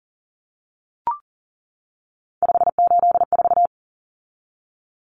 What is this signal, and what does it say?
Morse code, heard as a keyed tone, about 40 wpm, 700 Hz. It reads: 584
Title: Numbers - 3-Digit 40wpm